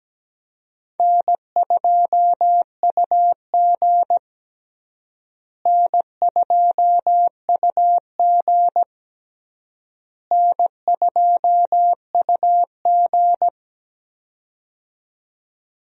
Morse code, heard as a keyed tone, about 17 wpm, 700 Hz. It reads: N2UG N2UG N2UG
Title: USA Call Signs - 4 Characters - Repeated 3x - 17wpm